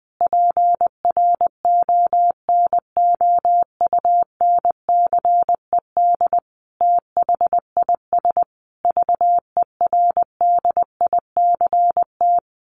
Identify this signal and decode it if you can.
Morse code, heard as a keyed tone, about 20 wpm, 700 Hz. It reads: PRONOUNCED THIS VERDICT
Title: A Sound Of Thunder 20wpm